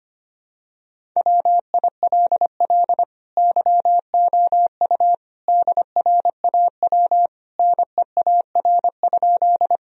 Morse code, heard as a keyed tone, about 25 wpm, 700 Hz. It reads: WILL YOU DRAW NEAR?
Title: Sentences from Top 1000 Words - Review Part 1 - 25wpm